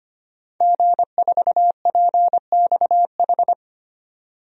Morse code, heard as a keyed tone, about 25 wpm, 700 Hz. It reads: G4PXH
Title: DXCC-100 Call Signs - 5 Characters - 25wpm